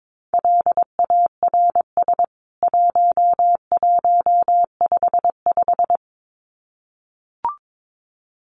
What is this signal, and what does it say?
Morse code, heard as a keyed tone, about 22 wpm, 700 Hz. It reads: LARS 1155
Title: CWT Exchange 22wpm